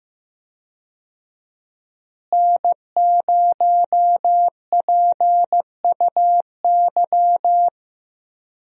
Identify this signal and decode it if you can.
Morse code, heard as a keyed tone, about 15 wpm, 700 Hz. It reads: N0PUY
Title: USA Call Signs - 5 Characters - Repeated 2x - 15wpm